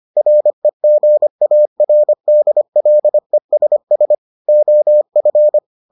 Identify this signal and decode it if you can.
Morse code, heard as a keyed tone, about 25 wpm, 600 Hz. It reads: REGARDLESS OF